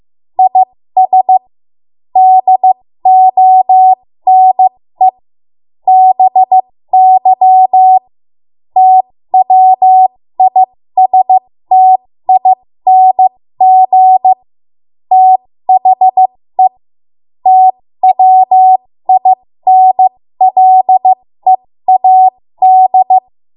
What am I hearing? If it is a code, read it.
Morse code, heard as a keyed tone, about 15 wpm, spaced as for 13 wpm, 750 Hz. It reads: IS DONE BY TWISTING THE TWINLEAD
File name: Morse / 13WPM